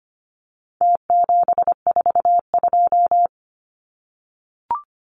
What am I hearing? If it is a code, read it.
Morse code, heard as a keyed tone, about 25 wpm, 700 Hz. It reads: T742
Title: Numbers - Contest Style - 4-Digit 25wpm